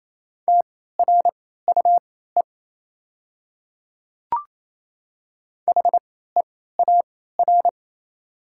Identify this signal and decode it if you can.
Morse code, heard as a keyed tone, about 28 wpm, spaced as for 16 wpm, 690 Hz. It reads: TRUE HEAR
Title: 3x Character Spacing - Top 500 Words - 28wpm